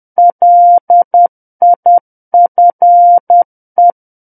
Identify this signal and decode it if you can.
Morse code, heard as a keyed tone, about 10 wpm, 700 Hz. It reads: LIFE
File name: Book One / 01-Chapter 1 - 10 WPM